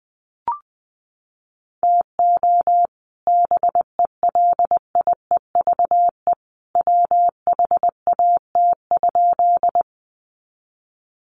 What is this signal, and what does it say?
Morse code, heard as a keyed tone, about 20 wpm, 700 Hz. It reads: TO BELIEVE WHAT?
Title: Sentences from Top 1000 Words 20wpm